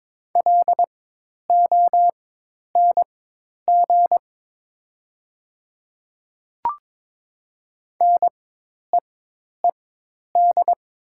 Morse code, heard as a keyed tone, about 22 wpm, spaced as for 10 wpm, 700 Hz. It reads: LONG NEED
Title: 4x Character Spacing - Top 500 Words - 22wpm